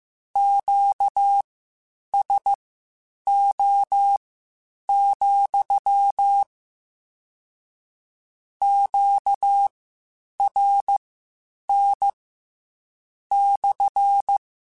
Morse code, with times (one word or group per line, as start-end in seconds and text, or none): 0.4-6.4 QSO,
8.6-12.1 QRN
13.3-14.4 /